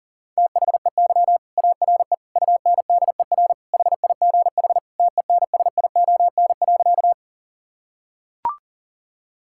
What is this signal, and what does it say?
Morse code, heard as a keyed tone, about 40 wpm, 700 Hz. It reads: THEY ARE UNDER HIGH TENSION.
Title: Sentences from Top 1600 Words 40wpm